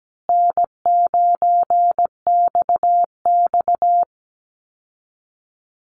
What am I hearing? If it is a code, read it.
Morse code, heard as a keyed tone, about 17 wpm, 700 Hz. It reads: N9XX
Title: Call Signs Repeated 2x - 17wpm